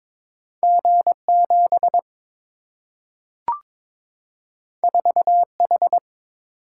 Morse code, heard as a keyed tone, about 22 wpm, 700 Hz. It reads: G 7 4 H
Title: Single-Letter Single-Number 22wpm